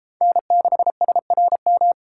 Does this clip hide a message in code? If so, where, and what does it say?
Yes: Morse code from 0.2 to 1.9 s, heard as a keyed tone, about 33 wpm, 700 Hz. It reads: N6SRM